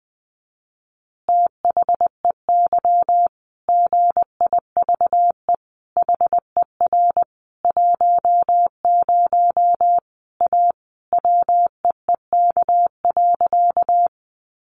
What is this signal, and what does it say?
Morse code, heard as a keyed tone, about 20 wpm, 700 Hz. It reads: THEY GIVE HER 10 A WEEK.
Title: Easy Phrases 20wpm